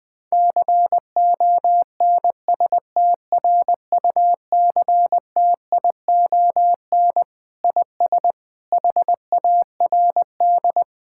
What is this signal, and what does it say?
Morse code, heard as a keyed tone, about 21 wpm, 700 Hz. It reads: CONSTRUCTION IS HARD
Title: Sentences from Top 1600 Words 20wpm